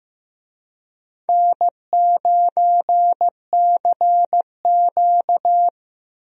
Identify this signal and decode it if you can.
Morse code, heard as a keyed tone, about 15 wpm, 700 Hz. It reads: N9CQ
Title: USA Call Signs - 4 Characters - Repeated 2x - 15wpm